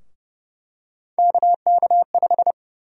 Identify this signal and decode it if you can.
Morse code, heard as a keyed tone, about 30 wpm, 700 Hz. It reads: KK5
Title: USA Call Sign Prefixes - 30wpm